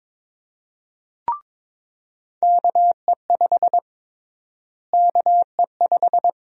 Morse code, heard as a keyed tone, about 22 wpm, 700 Hz. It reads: KE5 KE5
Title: USA Call Sign Prefixes - Repeated 2x - 22wpm